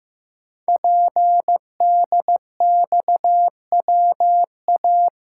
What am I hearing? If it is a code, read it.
Morse code, heard as a keyed tone, about 15 wpm, 700 Hz. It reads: PDXWA
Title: Code Groups - 5 Letter:Numbers - 15wpm